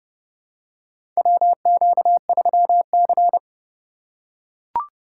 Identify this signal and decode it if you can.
Morse code, heard as a keyed tone, about 30 wpm, 700 Hz. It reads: WQ3C
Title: USA Call Signs - 4 Characters - 3x Character Spacing - 30wpm